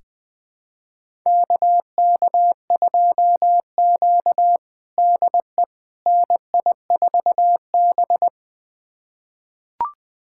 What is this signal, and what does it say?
Morse code, heard as a keyed tone, about 21 wpm, 700 Hz. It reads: KK2Q DE NI4B
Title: USA Callsign DE Callsign - 4 Characters - 20wpm